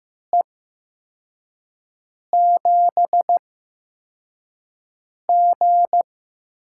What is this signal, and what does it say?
Morse code, heard as a keyed tone, about 15 wpm, spaced as for 4 wpm, 700 Hz. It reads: E7G
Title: USA Call Signs - 4 Characters - 8x Character Spacing - 15wpm